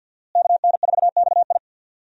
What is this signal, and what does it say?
Morse code, heard as a keyed tone, about 52 wpm, 700 Hz. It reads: KN4XI
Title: USA Call Signs - 5 Characters - Repeated 3x - 50wpm